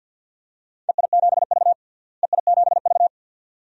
Morse code, heard as a keyed tone, about 47 wpm, 700 Hz. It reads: EI7V EI7V
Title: DXCC-100 Call Signs - 4 Characters - Repeated 2x - 50wpm